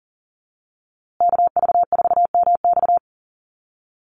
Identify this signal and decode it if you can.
Morse code, heard as a keyed tone, about 40 wpm, 700 Hz. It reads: KV4MX
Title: USA Call Signs Repeated 3x 40wpm